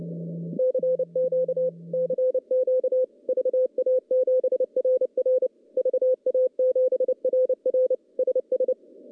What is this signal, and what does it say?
Morse code, heard as a keyed tone, about 30 wpm, 510 Hz. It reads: CQ CQ VA7RR VA7RR SS
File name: contests / CW-SS-k5tr- -0900